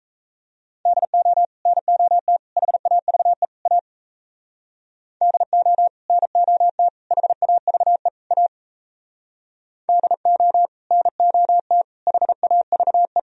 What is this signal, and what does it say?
Morse code, heard as a keyed tone, about 34 wpm, 700 Hz. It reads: DO NOT HAVE A DO NOT HAVE A DO NOT HAVE
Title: Speed-Racing - Sets of 4 Words 28wpm